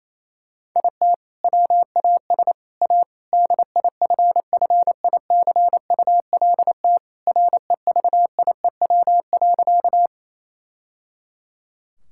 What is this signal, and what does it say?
Morse code, heard as a keyed tone, about 28 wpm, 700 Hz. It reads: IT WAS A DIFFICULT REVIEW.
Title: Sentences from Top 1000 Words - Review Part 6 - 28wpm